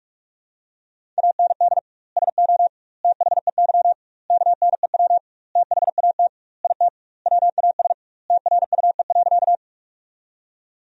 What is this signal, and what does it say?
Morse code, heard as a keyed tone, about 42 wpm, 700 Hz. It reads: AND SO THEY KNEW THAT IT WAS TRUE.
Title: 2x Word-Spacing - Sentences from Top 400 Words - 45wpm